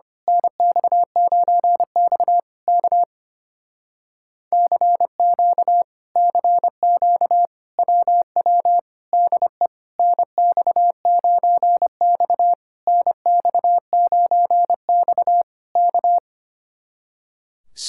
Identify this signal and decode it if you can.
Morse code, heard as a keyed tone, about 26 wpm, 700 Hz. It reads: NX9X K CQ CQ WW DE NX9X NX9X K
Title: Speed-Racing Common QSO Phrases - 25wpm